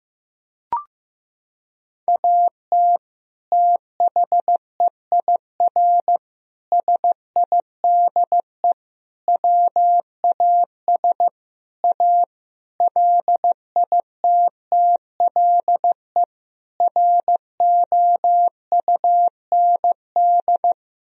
Morse code, heard as a keyed tone, about 15 wpm, 700 Hz. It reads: AT THEIR SIDE WAS A LITTLE ROUND
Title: Sentences from Top 1000 Words - Review Part 2 - 15wpm